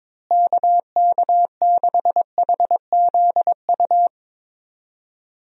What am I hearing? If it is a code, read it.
Morse code, heard as a keyed tone, about 22 wpm, 700 Hz. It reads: KK6HZU
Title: USA Call Signs - 6 Characters - Repeated 3x - 22wpm